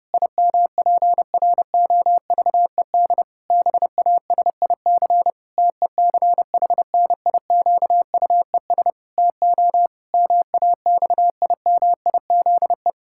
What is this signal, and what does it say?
Morse code, heard as a keyed tone, about 30 wpm, 700 Hz. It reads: IMPROVED BASIC TECHNIQUES TO MAXIMIZE
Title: Pangram Challenge v2 - 30wpm